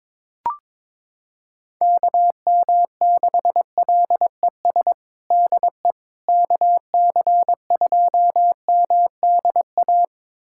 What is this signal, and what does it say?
Morse code, heard as a keyed tone, about 22 wpm, 700 Hz. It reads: KM6LES DE KC2MDA
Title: USA Callsign DE Callsign - 6 Characters - 22wpm